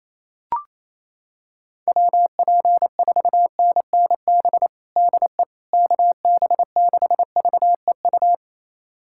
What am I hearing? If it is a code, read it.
Morse code, heard as a keyed tone, about 28 wpm, 700 Hz. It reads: WP4NNB DE KB6VEU
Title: USA Callsign DE Callsign - 6 Characters - 28wpm